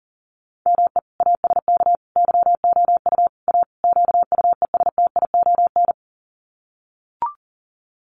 Morse code, heard as a keyed tone, about 40 wpm, 700 Hz. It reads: ME ASK YOU A QUESTION
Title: Sets of 5 Words Encore2 40wpm